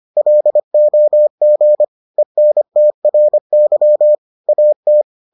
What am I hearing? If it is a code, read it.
Morse code, heard as a keyed tone, about 25 wpm, 600 Hz. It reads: LOG ENTRY AT